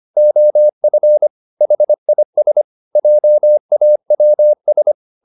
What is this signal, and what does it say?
Morse code, heard as a keyed tone, about 25 wpm, 600 Hz. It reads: OF HIS JAWS